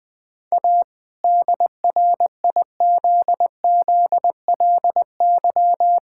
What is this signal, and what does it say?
Morse code, heard as a keyed tone, about 21 wpm, 700 Hz. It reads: A DRIZZLY
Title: All-Age Appropriate Jokes - 20wpm